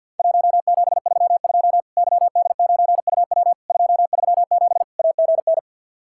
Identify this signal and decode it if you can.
Morse code, heard as a keyed tone, about 50 wpm, 700 Hz. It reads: 1732 YD9UW 237 AGN
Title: Worked All Europe DX Contest - QTC Exchange with Repeats - 50wpm